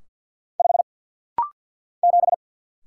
Morse code, heard as a keyed tone, about 49 wpm, 700 Hz. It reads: 5 7
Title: Single Letter-Number - Rapid-Fire 50wpm